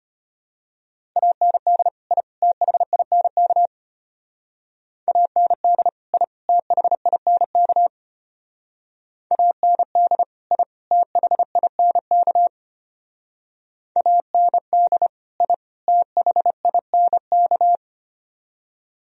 Morse code, heard as a keyed tone, about 32 wpm, 700 Hz. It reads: AND I THINK AND I THINK AND I THINK AND I THINK